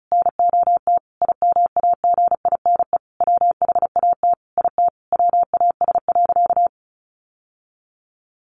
Morse code, heard as a keyed tone, about 34 wpm, 700 Hz. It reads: NOT IMAGINE WHAT IT WAS.
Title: Sentences from Top 1400 Words 35wpm